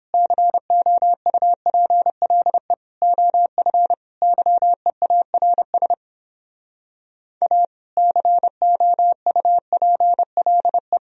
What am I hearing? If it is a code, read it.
Morse code, heard as a keyed tone, about 29 wpm, 700 Hz. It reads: COUPLE OF YEARS A COUPLE